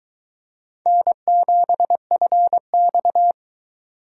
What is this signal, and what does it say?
Morse code, heard as a keyed tone, about 22 wpm, 700 Hz. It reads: N7FX